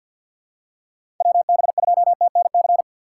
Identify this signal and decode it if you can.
Morse code, heard as a keyed tone, about 48 wpm, 700 Hz. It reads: WB2TNC
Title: USA Call Signs - 6 Characters - 6x Character Spacing - 50wpm